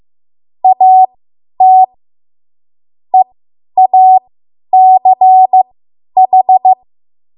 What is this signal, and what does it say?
Morse code, heard as a keyed tone, about 15 wpm, spaced as for 10 wpm, 750 Hz. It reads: AT EACH